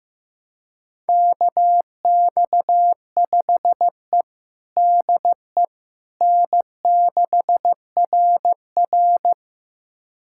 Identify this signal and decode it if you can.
Morse code, heard as a keyed tone, about 15 wpm, 700 Hz. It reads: KX5E DE N6RR
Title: USA Callsign DE Callsign - 4 Characters - 15wpm